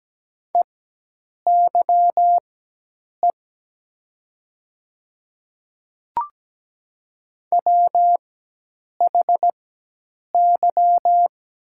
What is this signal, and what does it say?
Morse code, heard as a keyed tone, about 17 wpm, spaced as for 8 wpm, 690 Hz. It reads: EYE WHY
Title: 4x Character Spacing - Top 300 Words - 17wpm